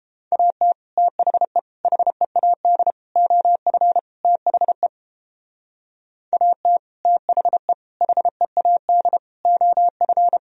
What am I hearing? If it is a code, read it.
Morse code, heard as a keyed tone, about 29 wpm, 700 Hz. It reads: AT THE HEAD OF THE AT THE HEAD OF